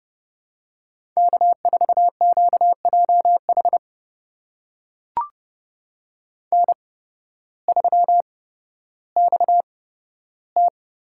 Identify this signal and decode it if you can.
Morse code, heard as a keyed tone, about 30 wpm, spaced as for 14 wpm, 700 Hz. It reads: K4QJH N3XT
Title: USA Call Signs - 5 Characters - 8x Character Spacing - 30wpm